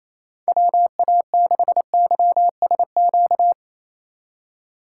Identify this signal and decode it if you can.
Morse code, heard as a keyed tone, about 34 wpm, 700 Hz. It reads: WA6YSQ
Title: Speed Racing - USA 2x3 Call Signs 28wpm